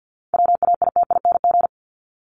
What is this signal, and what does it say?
Morse code, heard as a keyed tone, about 49 wpm, 700 Hz. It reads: WAITING
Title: ING Suffix - Top 100 Words with Base Words 50wpm